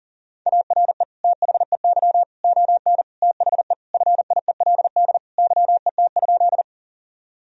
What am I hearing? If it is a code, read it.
Morse code, heard as a keyed tone, about 40 wpm, 700 Hz. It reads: ARE THEY ON THE FIELD YET?
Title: Sentences from Top 1000 Words - Review Part 2 - 40wpm